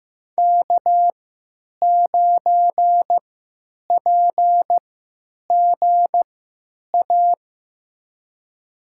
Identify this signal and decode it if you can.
Morse code, heard as a keyed tone, about 15 wpm, spaced as for 9 wpm, 700 Hz. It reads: K9PGA
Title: USA Call Signs - 3x Character Spacing - 15wpm